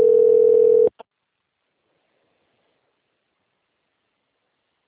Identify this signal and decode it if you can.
Ringback tone (North America), 440 + 480 Hz.